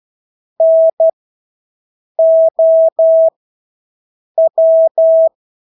Morse code, heard as a keyed tone, about 12 wpm, spaced as for 6 wpm, 650 Hz. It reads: NOW
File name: Engelske ord 12-6 wpm